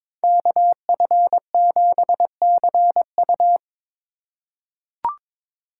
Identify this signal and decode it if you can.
Morse code, heard as a keyed tone, about 22 wpm, 700 Hz. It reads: KF7CU
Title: USA Call Signs - 5 Characters - 3x Character Spacing - 22wpm